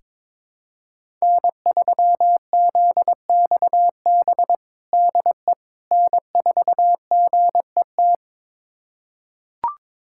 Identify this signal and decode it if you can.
Morse code, heard as a keyed tone, about 22 wpm, 700 Hz. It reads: N3ZXB DE N4GET
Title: USA Callsign DE Callsign - 5 Characters - 22wpm